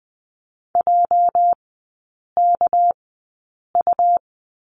Morse code, heard as a keyed tone, about 20 wpm, spaced as for 8 wpm, 700 Hz. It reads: JKU
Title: 205 Lesson 33e - Letter j - Three Characters 20wpm